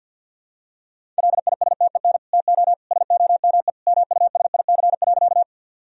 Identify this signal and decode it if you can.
Morse code, heard as a keyed tone, about 51 wpm, 700 Hz. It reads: LISTEN TO SOME MUSIC.
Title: Sentences from Top 1000 Words - Review Part 1 - 50wpm